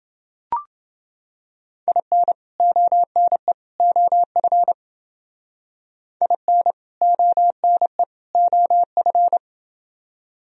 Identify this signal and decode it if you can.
Morse code, heard as a keyed tone, about 29 wpm, 700 Hz. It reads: IN ONE OF IN ONE OF